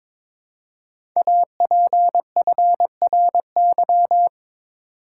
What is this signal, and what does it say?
Morse code, heard as a keyed tone, about 22 wpm, 700 Hz. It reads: APFRY
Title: Code Groups - 5 Letters - 22wpm